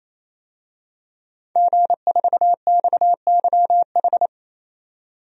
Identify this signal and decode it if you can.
Morse code, heard as a keyed tone, about 28 wpm, 700 Hz. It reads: G4XYH
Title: DXCC-100 Call Signs - 5 Characters - 28wpm